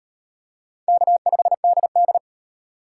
Morse code, heard as a keyed tone, about 37 wpm, 700 Hz. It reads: K5DD